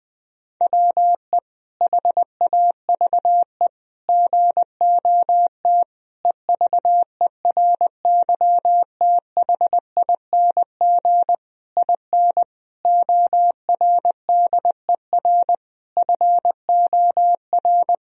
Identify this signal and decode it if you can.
Morse code, heard as a keyed tone, about 20 wpm, 700 Hz. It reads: WE HAVE GOT EVERYTHING IN ORDER FOR
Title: Sentences from Top 1000 Words - Review Part 3 - 20wpm